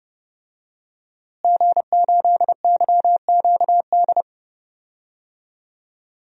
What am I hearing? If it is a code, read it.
Morse code, heard as a keyed tone, about 30 wpm, 700 Hz. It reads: G8YQD
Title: DXCC-100 Call Signs - 5 Characters - Repeated 2x - 30wpm